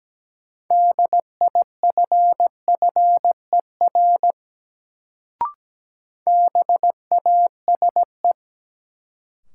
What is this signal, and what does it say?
Morse code, heard as a keyed tone, about 17 wpm, 700 Hz. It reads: DIFFER BASE
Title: Sentences from Top 1000 Words - Review Part 1 - New Word Drill - 17wpm